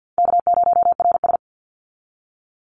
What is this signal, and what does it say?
Morse code, heard as a keyed tone, about 50 wpm, 700 Hz. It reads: K0RS
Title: NA SOTA Chaser Callsigns 50wpm